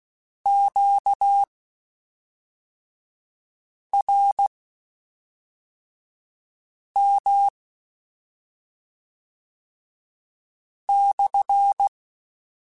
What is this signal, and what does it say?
Morse code, heard as a keyed tone, about 16 wpm, spaced as for 3 wpm, 800 Hz. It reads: QRM /
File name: Q Calls / Q Calls-mp3s / Q Calls SPSL2 @5